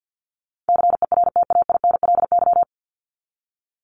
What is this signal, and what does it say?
Morse code, heard as a keyed tone, about 56 wpm, 700 Hz. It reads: CERTAINLY
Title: Top 751-1000 Words 50wpm